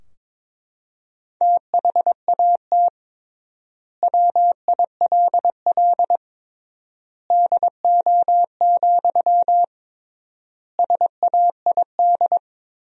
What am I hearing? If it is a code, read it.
Morse code, heard as a keyed tone, about 22 wpm, 700 Hz. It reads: THAT WILL DO, SAID
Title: 3x Word-Spacing - Sentences from Top 100 Words - 22wpm